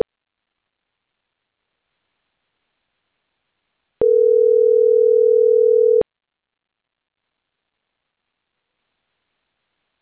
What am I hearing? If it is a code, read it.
Ringback tone (North America), 440 + 480 Hz.